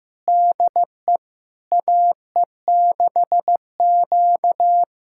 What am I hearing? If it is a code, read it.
Morse code, heard as a keyed tone, about 15 wpm, 700 Hz. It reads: DE AE6Q
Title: USA Callsign DE Callsign - 4 Characters - 15wpm